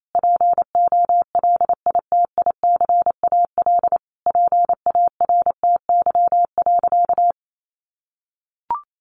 Morse code, heard as a keyed tone, about 28 wpm, 700 Hz. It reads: POLITICAL PARTY.
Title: Sentences from Top 1400 Words 28wpm